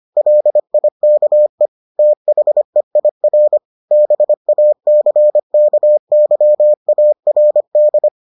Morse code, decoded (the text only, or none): LIKE THEIR BACKYARD